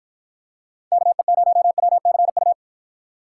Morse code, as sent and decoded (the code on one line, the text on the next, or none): -.- . ----- .-- -.-. ..-
KE0WCU